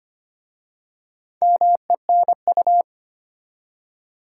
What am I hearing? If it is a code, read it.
Morse code, heard as a keyed tone, about 25 wpm, 700 Hz. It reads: MENU
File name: practice / 25wpm / Easy Words with Hints - 25wpm